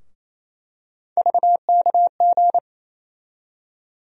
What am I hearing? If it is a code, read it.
Morse code, heard as a keyed tone, about 28 wpm, 700 Hz. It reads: VKG
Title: USA Call Sign Postfixes - 3 Letters 28wpm